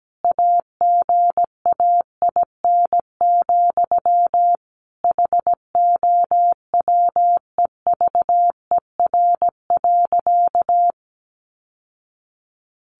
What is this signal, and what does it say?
Morse code, heard as a keyed tone, about 17 wpm, 700 Hz. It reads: AGAIN, HOWEVER.
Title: Sentences from Top 600 Words 17wpm